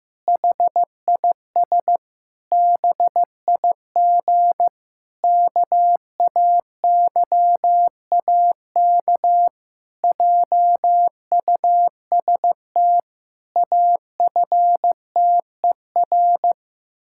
Morse code, heard as a keyed tone, about 15 wpm, 700 Hz. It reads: HIS BIG KAYAK JUST AFTER
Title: Pangram Challenge v2 - 15wpm